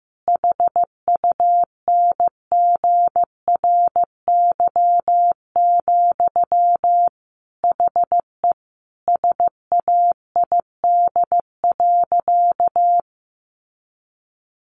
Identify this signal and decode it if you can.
Morse code, heard as a keyed tone, about 15 wpm, 700 Hz. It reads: HUNGRY, HE SAID.
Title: Sentences from Top 1400 Words 15wpm